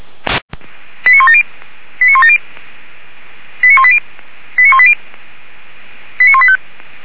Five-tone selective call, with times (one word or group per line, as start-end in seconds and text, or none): 1.1-1.4 89170
2.0-2.4 89170
3.6-4.0 89180
4.6-4.9 89180
6.2-6.6 89185